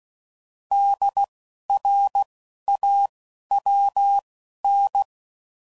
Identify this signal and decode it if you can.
Morse code, heard as a keyed tone, about 16 wpm, spaced as for 11 wpm, 800 Hz. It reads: DRAWN
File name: Prefixes / By First Letter / Ds